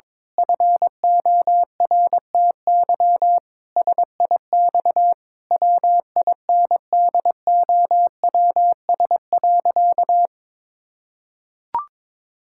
Morse code, heard as a keyed tone, about 22 wpm, 700 Hz. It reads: FORTY SIX WINDOWS.
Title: Pangram Challenge v2 - 22wpm